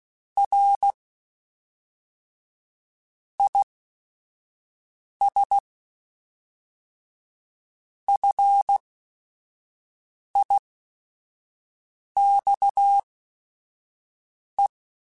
Morse code, heard as a keyed tone, about 16 wpm, spaced as for 4 wpm, 800 Hz. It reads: RIS FIXE
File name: QSOs / QSOs-mp3s / WA7QOX 1@5